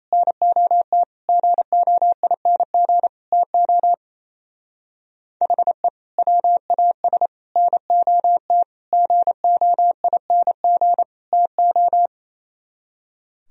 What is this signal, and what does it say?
Morse code, heard as a keyed tone, about 34 wpm, 700 Hz. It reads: NOT GOING TO HE WAS NOT GOING TO
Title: Speed-Racing - Sets of 5 Words 28wpm